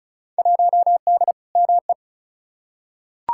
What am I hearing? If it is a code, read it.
Morse code, heard as a keyed tone, about 34 wpm, 700 Hz. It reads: 1D ME
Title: Field Day Exchanges - 35wpm